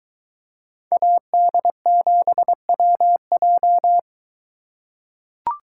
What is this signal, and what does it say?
Morse code, heard as a keyed tone, about 22 wpm, 700 Hz. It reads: AD7WJ